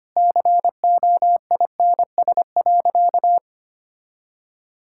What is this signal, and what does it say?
Morse code, heard as a keyed tone, about 25 wpm, 700 Hz. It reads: COINS.
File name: Easy Sentences with Hints - 25wpm